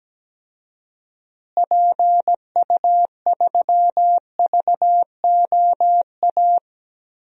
Morse code, heard as a keyed tone, about 17 wpm, 700 Hz. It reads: PU3VOA
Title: DXCC-100 Call Signs - 6 Characters - Repeated 3x - 17wpm